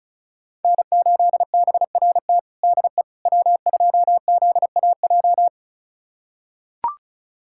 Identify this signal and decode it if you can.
Morse code, heard as a keyed tone, about 36 wpm, 700 Hz. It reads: N8BRT DE W2ZAJ
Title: USA Callsign DE Callsign - 5 Characters - 35wpm